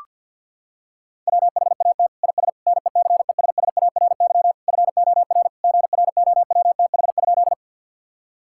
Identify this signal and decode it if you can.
Morse code, heard as a keyed tone, about 51 wpm, 700 Hz. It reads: WHAT IS NECESSARY FOR GROWTH?
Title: Sentences from Top 1000 Words - Review Part 5 - 50wpm